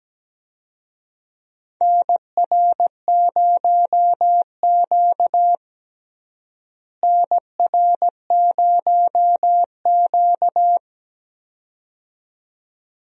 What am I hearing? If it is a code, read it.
Morse code, heard as a keyed tone, about 17 wpm, 700 Hz. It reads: NR0Q NR0Q
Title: USA Call Signs - 4 Characters - Repeated 2x - 17wpm